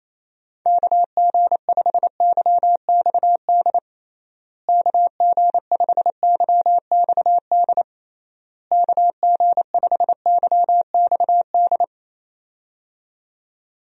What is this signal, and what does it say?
Morse code, heard as a keyed tone, about 28 wpm, 700 Hz. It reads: KG5YXD KG5YXD KG5YXD
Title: USA Call Signs - 6 Characters - Repeated 3x - 28wpm